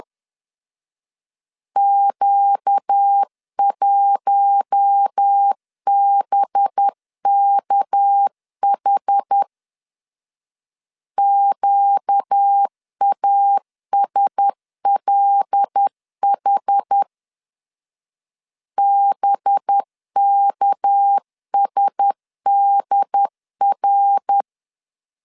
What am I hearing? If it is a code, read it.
Morse code, heard as a keyed tone, about 10 wpm, 800 Hz. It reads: Q1BKH QASLH BKSDR
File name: LEZIONI / 6 PKX0 / 6 PK 10wpm 2436